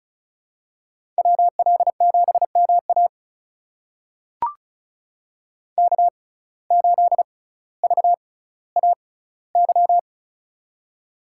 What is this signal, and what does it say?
Morse code, heard as a keyed tone, about 36 wpm, spaced as for 18 wpm, 700 Hz. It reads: WL7MA K8VAY
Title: USA Call Signs - 5 Characters - 6x Character Spacing - 35wpm